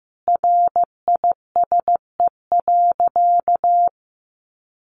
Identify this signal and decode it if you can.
Morse code, heard as a keyed tone, about 15 wpm, 700 Hz. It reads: RISE.
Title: Sentences from Top 1300 Words 15wpm